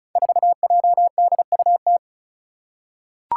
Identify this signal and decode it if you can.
Morse code, heard as a keyed tone, about 34 wpm, 700 Hz. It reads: 4JDUT
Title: Five Challenging Letter-Number Combinations 35wpm